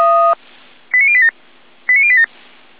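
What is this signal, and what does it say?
Five-tone selective call (ZVEI), heard as quick runs of five tones, tones of about 70 ms. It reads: 89087 89087
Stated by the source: ZVEI five-tone call 89080, received by SDR.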